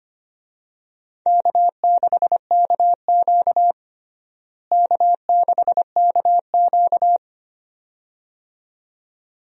Morse code, heard as a keyed tone, about 25 wpm, 700 Hz. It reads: K6KQ K6KQ
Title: USA Call Signs - 4 Characters - Repeated 2x - 25wpm